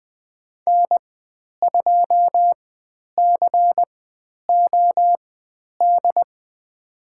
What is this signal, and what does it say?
Morse code, heard as a keyed tone, about 20 wpm, spaced as for 10 wpm, 700 Hz. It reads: N2COD
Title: 109 Lesson 18g - Letter m - Call Signs 20wpm